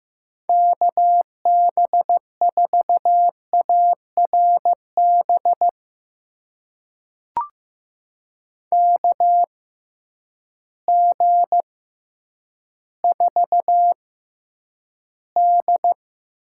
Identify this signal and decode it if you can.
Morse code, heard as a keyed tone, about 15 wpm, spaced as for 9 wpm, 700 Hz. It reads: KB4ARB KG4D
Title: USA Call Signs - 6 Characters - 6x Character Spacing - 15wpm